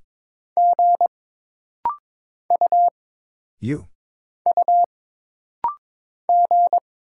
Morse code, heard as a keyed tone, about 22 wpm, 700 Hz. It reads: G U U G
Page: This rapid-fire timing is geared to push your ICR (Instant Character Recognition) skills.